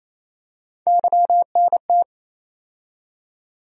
Morse code, heard as a keyed tone, about 27 wpm, 700 Hz. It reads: YNT
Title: USA Call Sign Postfixes - 3 Letters 28wpm